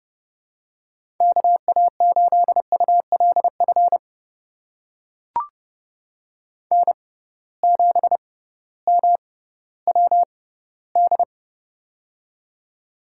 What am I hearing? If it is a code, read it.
Morse code, heard as a keyed tone, about 31 wpm, spaced as for 16 wpm, 700 Hz. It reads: KA8ULF N7MWD
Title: USA Call Signs - 6x Character Spacing - 30wpm